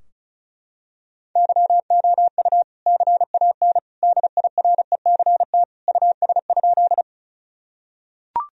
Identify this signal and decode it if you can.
Morse code, heard as a keyed tone, about 34 wpm, 700 Hz. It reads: YOU CAN DIRECT US?
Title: Sentences from Top 1000 Words - Review Part 2 - 35wpm